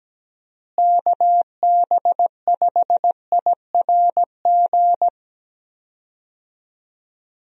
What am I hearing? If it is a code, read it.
Morse code, heard as a keyed tone, about 17 wpm, 700 Hz. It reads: KB5IRG
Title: USA Call Signs - 6 Characters - Repeated 3x - 17wpm